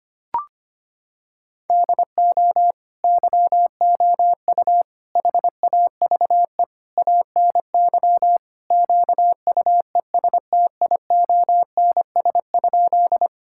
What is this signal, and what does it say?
Morse code, heard as a keyed tone, about 25 wpm, 700 Hz. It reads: DO YOU HAVE ANY QUESTIONS?
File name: practice / 25wpm / Sentences with 4-Word Phrases - 25wpm